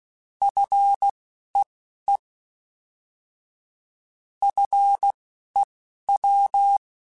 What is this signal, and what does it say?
Morse code, heard as a keyed tone, about 16 wpm, spaced as for 11 wpm, 800 Hz. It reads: FEE FEW
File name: Alphabetical / Alphabetical mp3s / Fs @10